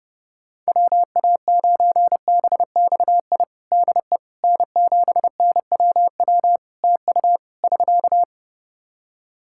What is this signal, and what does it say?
Morse code, heard as a keyed tone, about 28 wpm, 700 Hz. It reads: WA9BXI DE N7NWW TU <SK>
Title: Practice QSO - Exchange 4 - Ending - 30wpm